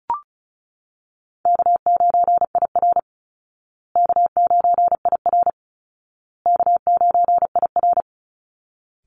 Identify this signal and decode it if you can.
Morse code, heard as a keyed tone, about 34 wpm, 700 Hz. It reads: K9IR K9IR K9IR
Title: Call Signs Repeated 2x - 35wpm